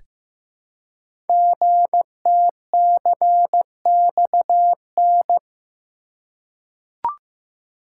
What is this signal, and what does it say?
Morse code, heard as a keyed tone, about 15 wpm, 700 Hz. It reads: GTCXN
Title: Code Groups - 5 Letters - 15wpm